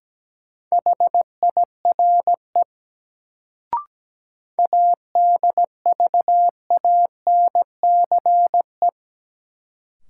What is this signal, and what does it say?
Morse code, heard as a keyed tone, about 17 wpm, 700 Hz. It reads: HIRE ADVANCE
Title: Sentences from Top 1400 Words - New Word Drill Alt - 17wpm